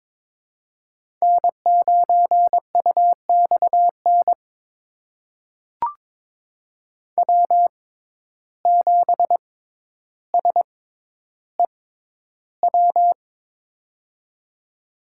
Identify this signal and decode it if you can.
Morse code, heard as a keyed tone, about 23 wpm, spaced as for 11 wpm, 700 Hz. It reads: N9UXN W7SEW
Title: USA Call Signs - 5 Characters - 6x Character Spacing - 22wpm